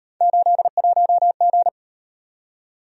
Morse code, heard as a keyed tone, about 38 wpm, 700 Hz. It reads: 81G